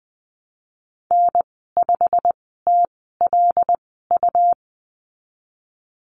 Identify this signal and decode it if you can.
Morse code, heard as a keyed tone, about 20 wpm, spaced as for 15 wpm, 700 Hz. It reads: N5TLU
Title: 128 Lesson 21g - Number 6 - Call Signs 20wpm